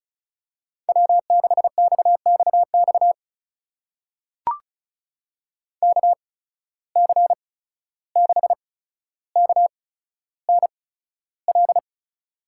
Morse code, heard as a keyed tone, about 36 wpm, spaced as for 14 wpm, 700 Hz. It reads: W6XXX KC6KNL
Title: USA Call Signs - 8x Character Spacing - 35wpm